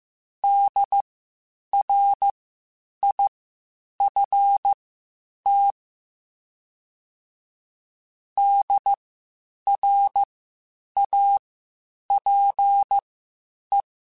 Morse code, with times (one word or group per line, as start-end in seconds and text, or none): 0.4-5.7 DRIFT
8.4-13.8 DRAPE